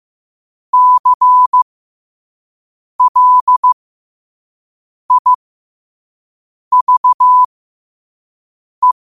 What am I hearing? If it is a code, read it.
Morse code, heard as a keyed tone, about 15 wpm, spaced as for 5 wpm, 1000 Hz. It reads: CLIVE
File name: qotdfiles / 5wpm